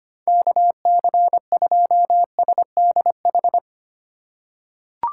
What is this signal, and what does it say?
Morse code, heard as a keyed tone, about 25 wpm, 700 Hz. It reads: KC2SDH
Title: USA Call Signs - 6 Characters - 4x Character Spacing - 25wpm